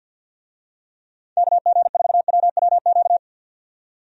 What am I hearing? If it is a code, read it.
Morse code, heard as a keyed tone, about 51 wpm, 700 Hz. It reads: KG4WWQ
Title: USA Call Signs - 6 Characters - 50wpm